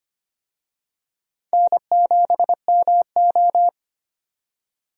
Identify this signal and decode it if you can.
Morse code, heard as a keyed tone, about 25 wpm, 700 Hz. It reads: N7MO
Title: USA Call Signs - 4 Characters - 25wpm